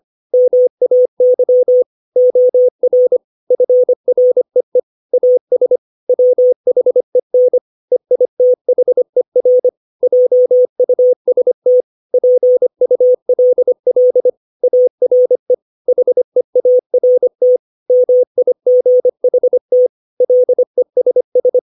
Morse code, heard as a keyed tone, about 25 wpm, 500 Hz. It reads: MAY OR FREE AS WHEN EITHER JUST PULL ARE HEART MIGHT LESS
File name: 632 words / 632 Words 25WPM